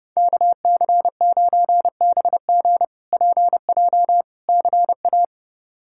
Morse code, heard as a keyed tone, about 32 wpm, 700 Hz. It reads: KC9BG PJ CA
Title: K1USN SST QSO Practice - 30wpm